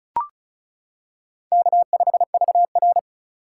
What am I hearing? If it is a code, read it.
Morse code, heard as a keyed tone, about 36 wpm, 700 Hz. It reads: K5VR
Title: K1USN SST QSO Practice - 35wpm